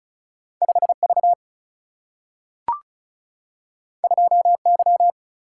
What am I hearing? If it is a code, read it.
Morse code, heard as a keyed tone, about 36 wpm, 700 Hz. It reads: 5V 2Y
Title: Single-Letter Single-Number 35wpm